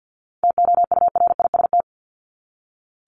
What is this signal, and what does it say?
Morse code, heard as a keyed tone, about 48 wpm, 700 Hz. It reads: TOURIST
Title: Top 1501-1750 Words 50wpm